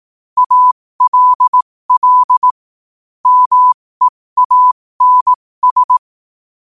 Morse code, heard as a keyed tone, about 18 wpm, spaced as for 15 wpm, 1000 Hz. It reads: ALL MEANS
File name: qotdfiles / 15wpm